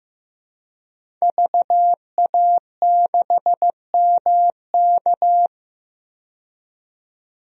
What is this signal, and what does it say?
Morse code, heard as a keyed tone, about 16 wpm, 700 Hz. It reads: VA6MK
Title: DXCC-100 Call Signs - 5 Characters - 15wpm